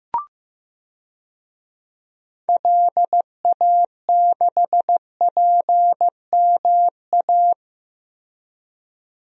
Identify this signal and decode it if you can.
Morse code, heard as a keyed tone, about 15 wpm, 700 Hz. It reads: LA6PMA
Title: DXCC-100 Call Signs - 6 Characters - Repeated 2x - 15wpm